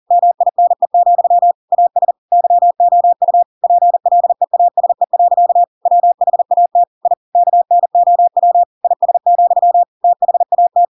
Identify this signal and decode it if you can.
Morse code, heard as a keyed tone, about 40 wpm, 700 Hz. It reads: MINE, AS YOU PLEASE. WHAT I KNOW IS, THAT
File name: Pinocchio / Pinocchio ch4 40 40 ik1hge